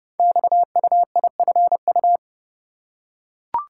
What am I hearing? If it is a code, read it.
Morse code, heard as a keyed tone, about 30 wpm, 700 Hz. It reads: XUIFU
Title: Code Groups - 5 Letters - 30wpm